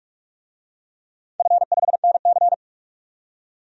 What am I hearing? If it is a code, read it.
Morse code, heard as a keyed tone, about 42 wpm, 700 Hz. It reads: F5NC
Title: DXCC-100 Call Signs - 4 Characters - 45wpm